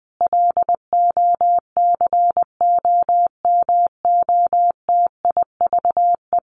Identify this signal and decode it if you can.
Morse code, heard as a keyed tone, about 20 wpm, 700 Hz. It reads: LOCOMOTIVE
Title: A Sound Of Thunder 20wpm